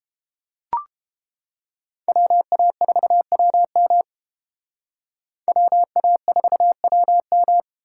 Morse code, heard as a keyed tone, about 29 wpm, 700 Hz. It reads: WA4WM WA4WM